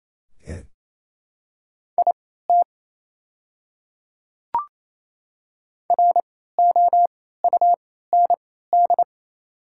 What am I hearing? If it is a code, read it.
Morse code, heard as a keyed tone, about 28 wpm, spaced as for 16 wpm, 700 Hz. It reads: IT ROUND
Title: 3x Character Spacing - Top 200 Words - 28wpm